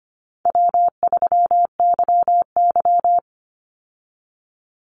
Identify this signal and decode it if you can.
Morse code, heard as a keyed tone, about 25 wpm, 700 Hz. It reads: W3YY
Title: Call Signs Repeated 2x - 25wpm